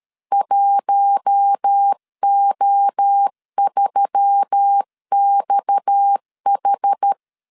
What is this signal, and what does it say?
Morse code, heard as a keyed tone, about 13 wpm, 800 Hz. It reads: 1O3XH
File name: LEZIONI / 7 GF9Z / 7 GF9Z 12wpm 2147